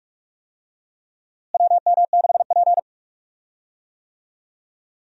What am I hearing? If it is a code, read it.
Morse code, heard as a keyed tone, about 46 wpm, 700 Hz. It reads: WM6P
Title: USA Call Signs - 4 Characters - Repeated 3x - 45wpm